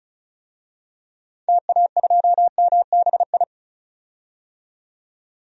Morse code, heard as a keyed tone, about 36 wpm, 690 Hz. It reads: TA2MBI
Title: DXCC-100 Call Signs - 6 Characters - 35wpm